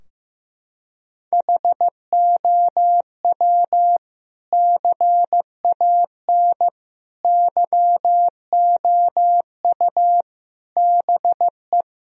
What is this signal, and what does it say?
Morse code, heard as a keyed tone, about 15 wpm, 700 Hz. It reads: HOW CAN YOU BE
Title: Sentences from Top 1000 Words - Review Part 2 - 15wpm